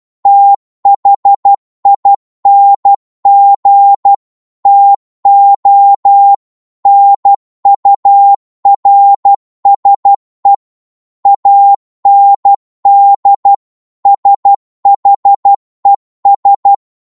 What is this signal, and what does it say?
Morse code, heard as a keyed tone, about 12 wpm, 800 Hz. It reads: THING TO NURSE AND SHES
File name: Morse / psg-12wpm